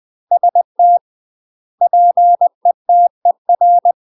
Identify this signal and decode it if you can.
Morse code, heard as a keyed tone, about 20 wpm, 700 Hz. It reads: ST PETER
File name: UK MASTER QSOs TRAINING / 20WPM / UK MASTER QSOs 0067